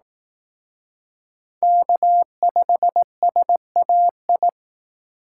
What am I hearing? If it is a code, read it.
Morse code, heard as a keyed tone, about 18 wpm, 700 Hz. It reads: K5SAI
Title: Speed Racing - USA 1x3 Call Signs 15wpm